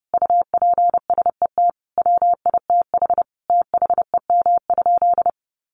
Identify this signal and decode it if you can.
Morse code, heard as a keyed tone, about 30 wpm, 700 Hz. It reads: UPSET WITH THEM?
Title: Sentences from Top 800 Words 30wpm